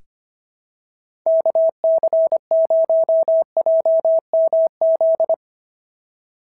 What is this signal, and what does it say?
Morse code, heard as a keyed tone, about 24 wpm, 650 Hz. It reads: KC0JMZ
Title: SOTA QSO Practice 25wpm